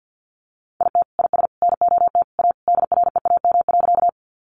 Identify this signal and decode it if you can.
Morse code, heard as a keyed tone, about 49 wpm, 700 Hz. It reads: IT IS NOT A DREAM.
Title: Sentences from Top 1500 Words 50wpm